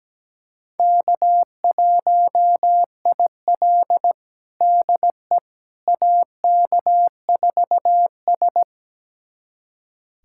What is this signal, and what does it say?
Morse code, heard as a keyed tone, about 17 wpm, 700 Hz. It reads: K1IL DE AK4S
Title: USA Callsign DE Callsign - 4 Characters - 17wpm